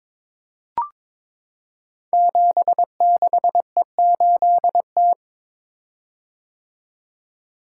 Morse code, heard as a keyed tone, about 22 wpm, 700 Hz. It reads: 76E8T
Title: Code Groups - 5 Letter:Numbers - 22wpm